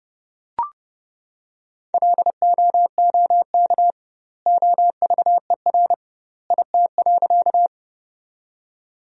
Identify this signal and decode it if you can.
Morse code, heard as a keyed tone, about 32 wpm, 700 Hz. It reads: LOOK OVER IT.
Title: 2x Word-Spacing - Sentences from Top 100 Words - 30wpm